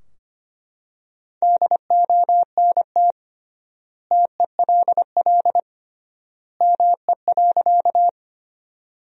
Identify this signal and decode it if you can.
Morse code, heard as a keyed tone, about 25 wpm, 700 Hz. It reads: DONT TELL ME.
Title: 3x Word-Spacing - Sentences from Top 300 Words - 25wpm